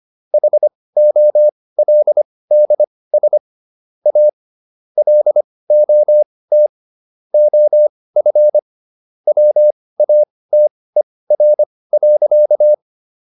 Morse code, heard as a keyed tone, about 25 wpm, spaced as for 18 wpm, 600 Hz. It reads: HOLDS A LOT OF WATER.